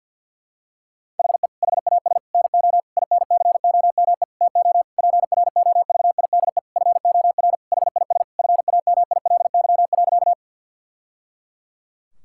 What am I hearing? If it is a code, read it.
Morse code, heard as a keyed tone, about 51 wpm, 690 Hz. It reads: HE HAS NO INCOME TO PROVIDE FOR HIS FAMILY.
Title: Sentences from Top 1000 Words - Review Part 5 - 50wpm